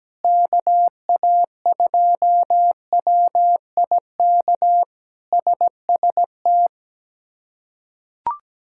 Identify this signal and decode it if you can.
Morse code, heard as a keyed tone, about 17 wpm, 700 Hz. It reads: KA2WIK SST
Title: K1USN SST QSO Practice - 17wpm